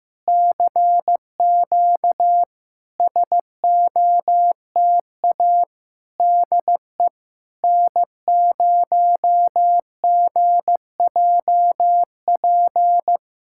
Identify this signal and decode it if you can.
Morse code, heard as a keyed tone, about 15 wpm, 700 Hz. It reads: CQ SOTA DE N0GJP
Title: SOTA QSO Practice 15wpm